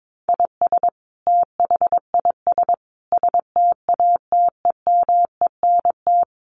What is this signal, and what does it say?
Morse code, heard as a keyed tone, about 22 wpm, 700 Hz. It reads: IS THIS STATEMENT
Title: Sentences from Top 1200 Words 22wpm